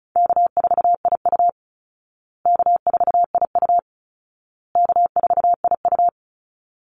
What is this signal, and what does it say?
Morse code, heard as a keyed tone, about 34 wpm, 700 Hz. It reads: K4IU K4IU K4IU
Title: Easy Call Signs Repeated 3x - 35wpm